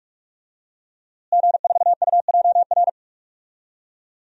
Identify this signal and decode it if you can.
Morse code, heard as a keyed tone, about 41 wpm, 700 Hz. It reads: G4UJR
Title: DXCC-100 Call Signs - 5 Characters - Repeated 2x - 45wpm